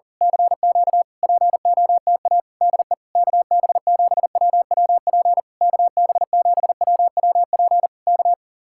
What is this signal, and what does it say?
Morse code, heard as a keyed tone, about 38 wpm, 700 Hz. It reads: CQ POTA DE KB7WWP KB7WWP K
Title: Speed-Racing Common QSO Phrases Encore - 40wpm